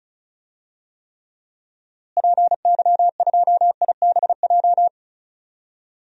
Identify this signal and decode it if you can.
Morse code, heard as a keyed tone, about 34 wpm, 700 Hz. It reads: PY2IBJ
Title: DXCC-100 Call Signs - 6 Characters - Repeated 3x - 35wpm